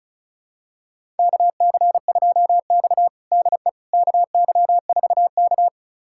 Morse code, heard as a keyed tone, about 36 wpm, 700 Hz. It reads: KC2X DE KY4K
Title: USA Callsign DE Callsign - 4 Characters - 35wpm